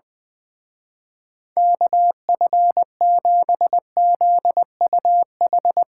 Morse code, heard as a keyed tone, about 22 wpm, 700 Hz. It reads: KF7ZUH
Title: Speed Racing - USA 2x3 Call Signs 20wpm